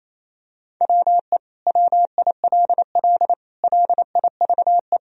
Morse code, heard as a keyed tone, about 28 wpm, 700 Hz. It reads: WE WILL LIVE
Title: Sentences from Top 1000 Words - Review Part 4 - 28wpm